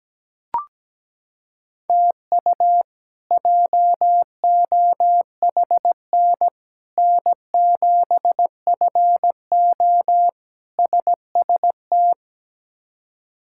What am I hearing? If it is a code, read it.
Morse code, heard as a keyed tone, about 17 wpm, 700 Hz. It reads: TU JOHN N7FO SST
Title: K1USN SST QSO Practice - 17wpm